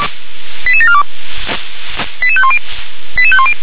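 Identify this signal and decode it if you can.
Five-tone selective call (ZVEI), heard as quick runs of five tones, tones of about 70 ms. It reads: 88742 88510 88510
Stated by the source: ZVEI five-tone selective call received on an SDR.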